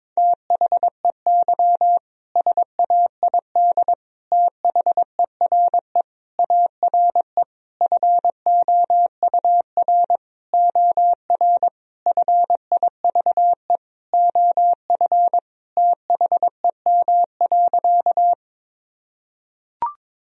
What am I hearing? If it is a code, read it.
Morse code, heard as a keyed tone, about 22 wpm, 700 Hz. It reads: THEY SAID THERE ARE FOUR OR FIVE OF THEM.
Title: Sentences from Top 1000 Words - Review Part 2 - 22wpm